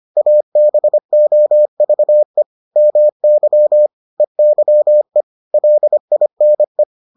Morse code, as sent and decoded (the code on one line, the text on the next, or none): .- -... --- ...- . / -- -.-- / . -.-- . / .-.. .. -. .
ABOVE MY EYE LINE